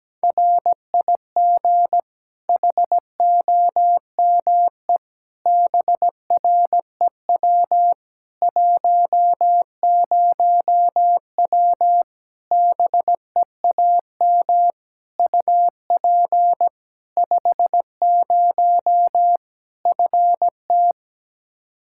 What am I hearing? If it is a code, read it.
Morse code, heard as a keyed tone, about 17 wpm, 700 Hz. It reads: RIG HOME BREW 10W BEAM UP 50 FT
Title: Practice QSO - Exchange 2 - Info - 17wpm